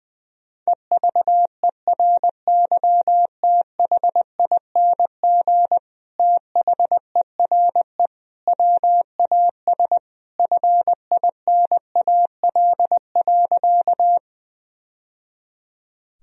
Morse code, heard as a keyed tone, about 21 wpm, 700 Hz. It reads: EVERYTHING THERE WAS FINAL.
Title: Sentences from Top 1000 Words - Review Part 2 - 20wpm